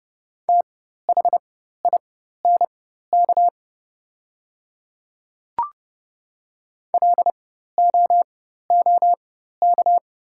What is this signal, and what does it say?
Morse code, heard as a keyed tone, about 30 wpm, spaced as for 14 wpm, 700 Hz. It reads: THINK LOOK
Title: 4x Character Spacing - Top 300 Words - 30wpm